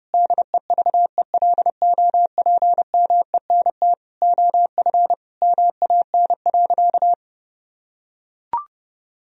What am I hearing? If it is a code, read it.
Morse code, heard as a keyed tone, about 31 wpm, 700 Hz. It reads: DEVELOPMENT OF MAN.
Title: Sentences from Top 1000 Words - Review Part 4 - 30wpm